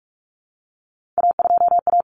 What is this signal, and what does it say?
Morse code, heard as a keyed tone, about 44 wpm, 700 Hz. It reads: A2A
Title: Common Cut Numbers - 3-Digit 45wpm